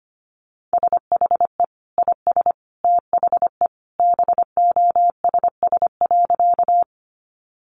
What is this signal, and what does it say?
Morse code, heard as a keyed tone, about 25 wpm, 700 Hz. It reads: SHE IS THE BOSS.
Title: Sentences from Top 700 Words 25wpm